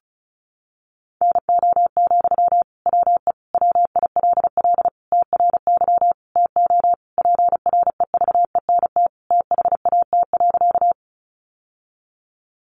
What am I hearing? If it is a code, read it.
Morse code, heard as a keyed tone, about 34 wpm, 700 Hz. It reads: NO, WE WILL TRY TO PREVENT THAT.
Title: Sentences from Top 1100 Words 35wpm